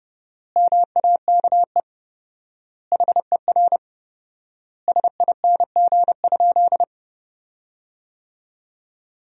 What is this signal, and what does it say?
Morse code, heard as a keyed tone, about 30 wpm, 700 Hz. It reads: MAKE HER SING?
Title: 4x Word-Spacing - Sentences from Top 400 Words - 30wpm